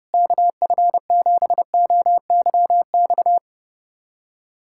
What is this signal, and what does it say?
Morse code, heard as a keyed tone, about 30 wpm, 700 Hz. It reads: KF7OYX
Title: USA Callsign DE Callsign - 6 Characters - 30wpm